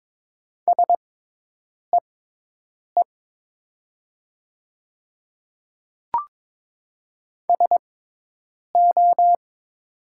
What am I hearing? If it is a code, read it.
Morse code, heard as a keyed tone, about 22 wpm, spaced as for 8 wpm, 700 Hz. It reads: SEE SO
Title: 6x Character Spacing - Top 100 Words - 22wpm